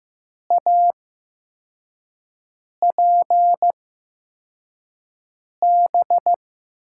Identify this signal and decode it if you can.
Morse code, heard as a keyed tone, about 15 wpm, spaced as for 4 wpm, 700 Hz. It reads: APB
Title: USA Call Signs - 6 Characters - 8x Character Spacing - 15wpm